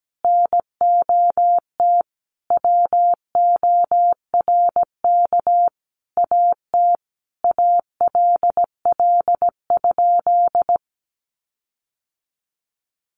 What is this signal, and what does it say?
Morse code, heard as a keyed tone, about 17 wpm, 700 Hz. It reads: NOT WORK AT ALL?
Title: Sentences from Top 1500 Words 17wpm